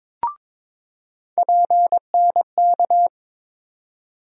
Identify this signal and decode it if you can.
Morse code, heard as a keyed tone, about 22 wpm, 700 Hz. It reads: PNK
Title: USA Call Sign Postfixes - 3 Letters 22wpm